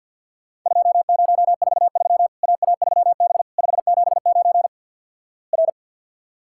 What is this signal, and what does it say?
Morse code, heard as a keyed tone, about 50 wpm, 700 Hz. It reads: 2043 AA3B 579 R
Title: Worked All Europe DX Contest - QTC Exchange - 50wpm